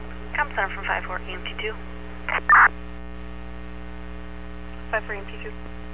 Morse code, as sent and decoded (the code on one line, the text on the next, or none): none